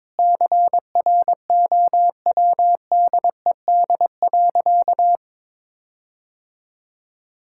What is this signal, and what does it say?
Morse code, heard as a keyed tone, about 22 wpm, 700 Hz. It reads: CROWDED.
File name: Sentences with 7-Word Phrases - 22wpm